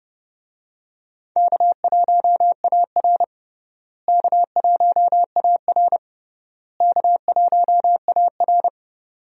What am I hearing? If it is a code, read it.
Morse code, heard as a keyed tone, about 30 wpm, 700 Hz. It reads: K1AR K1AR K1AR
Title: USA Call Signs - 4 Characters - Repeated 3x - 30wpm